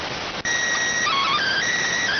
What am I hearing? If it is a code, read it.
SSTV (slow-scan television), PD 290.